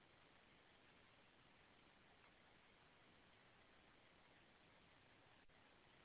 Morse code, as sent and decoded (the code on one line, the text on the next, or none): none